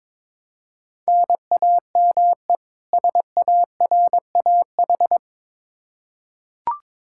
Morse code, heard as a keyed tone, about 22 wpm, 700 Hz. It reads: NAME SARAH
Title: K1USN SST QSO Practice 22wpm